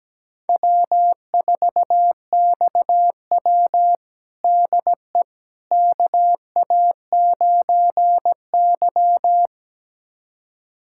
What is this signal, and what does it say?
Morse code, heard as a keyed tone, about 17 wpm, 700 Hz. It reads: W4XW DE KA9Y
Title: USA Callsign DE Callsign - 4 Characters - 17wpm